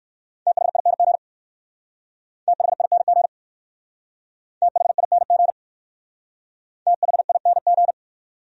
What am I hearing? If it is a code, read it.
Morse code, heard as a keyed tone, about 56 wpm, 700 Hz. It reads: THING THING THING THING
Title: Speed-Racing Top 300 Words 45wpm